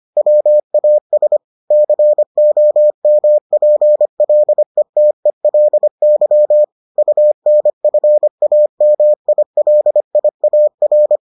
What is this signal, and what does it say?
Morse code, heard as a keyed tone, about 25 wpm, 600 Hz. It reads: WAS COMPLETELY UNFAMILIAR